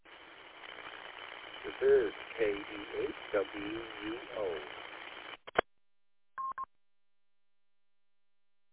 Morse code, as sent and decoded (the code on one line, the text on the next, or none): none